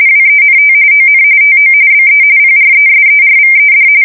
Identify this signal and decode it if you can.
RTTY, about 45.45 baud. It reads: WENT ON TO BE PRESERVED
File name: fsk170 / USS-New-Jersey